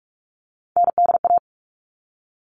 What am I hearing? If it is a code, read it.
Morse code, heard as a keyed tone, about 44 wpm, 700 Hz. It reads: NDA
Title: All Cut Numbers - 3-Digit 45wpm